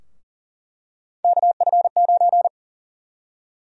Morse code, heard as a keyed tone, about 40 wpm, 700 Hz. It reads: KF9
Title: USA Call Sign Prefixes - 40wpm